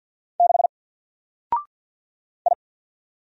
Morse code, heard as a keyed tone, about 48 wpm, 700 Hz. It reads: 6 I
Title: Single Character (Letter:Number:Punctuation:Prosign) 50wpm